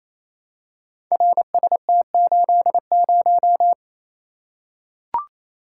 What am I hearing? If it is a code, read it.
Morse code, heard as a keyed tone, about 28 wpm, 700 Hz. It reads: RST80
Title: Code Groups - 5 Letter:Numbers - 28wpm